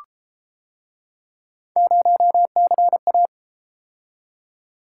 Morse code, heard as a keyed tone, about 32 wpm, 700 Hz. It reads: 0CA